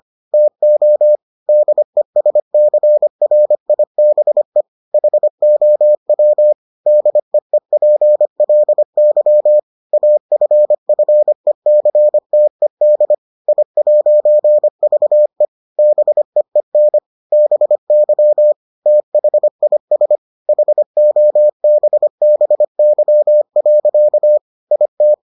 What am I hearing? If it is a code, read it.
Morse code, heard as a keyed tone, about 25 wpm, 600 Hz. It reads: TO DESCRIBE HOW DEEPLY AFFECTED I'VE BEEN BY THIS HOBBY. IT